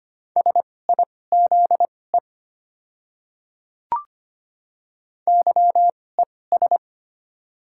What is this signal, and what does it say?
Morse code, heard as a keyed tone, about 25 wpm, spaced as for 18 wpm, 700 Hz. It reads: SIZE YES
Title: 2x Character Spacing - Top 500 Words - 25wpm